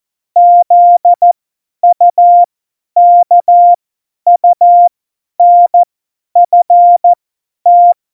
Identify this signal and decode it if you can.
Morse code, heard as a keyed tone, about 14 wpm, spaced as for 10 wpm, 700 Hz. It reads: ZUKUNFT
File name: Morsekurs / Witze / Witz 20 March 2018, 50 F70